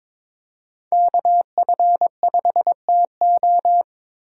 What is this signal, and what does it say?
Morse code, heard as a keyed tone, about 22 wpm, 700 Hz. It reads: KF5TO
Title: USA Call Signs - 5 Characters - 3x Character Spacing - 22wpm